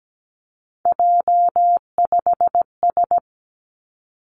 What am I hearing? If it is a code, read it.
Morse code, heard as a keyed tone, about 17 wpm, 700 Hz. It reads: J5S
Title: Three Challenging Letter-Number Combinations 17wpm